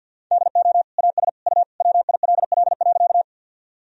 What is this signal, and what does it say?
Morse code, heard as a keyed tone, about 47 wpm, 700 Hz. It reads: DO AS U WILL.
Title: Sentences from Top 100 Words and Abbreviations 50wpm